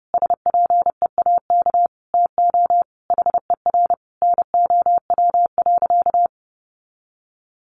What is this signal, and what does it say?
Morse code, heard as a keyed tone, about 30 wpm, 700 Hz. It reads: SPEAK TO HER NOW.
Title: Sentences from Top 1400 Words 30wpm